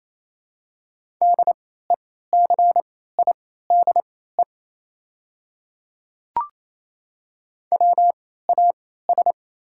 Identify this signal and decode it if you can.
Morse code, heard as a keyed tone, about 27 wpm, spaced as for 16 wpm, 700 Hz. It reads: DECIDE WAS
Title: 3x Character Spacing - Top 500 Words - 28wpm